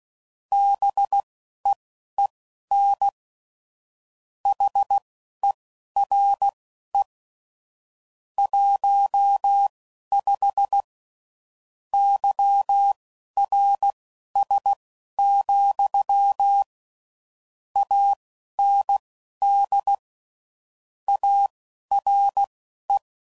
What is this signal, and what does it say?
Morse code, heard as a keyed tone, about 16 wpm, spaced as for 11 wpm, 800 Hz. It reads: BEEN HERE 15 YRS, AND ARE
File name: CW Code Groups @15WPM / QSOs / WD0BGP Spaced